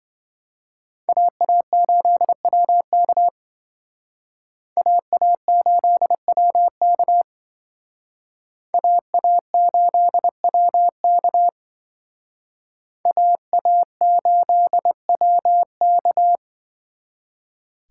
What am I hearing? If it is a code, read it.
Morse code, heard as a keyed tone, about 26 wpm, 700 Hz. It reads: AA8WK AA8WK AA8WK AA8WK